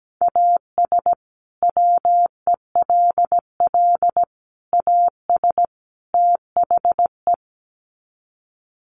Morse code, heard as a keyed tone, about 17 wpm, 700 Hz. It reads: AS WELL AS THE
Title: Sets of 4 Words 17wpm